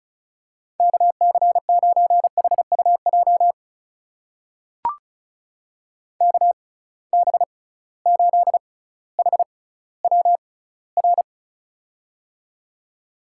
Morse code, heard as a keyed tone, about 36 wpm, spaced as for 18 wpm, 700 Hz. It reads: KC9HUJ KB8HWR
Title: USA Call Signs - 6 Characters - 6x Character Spacing - 35wpm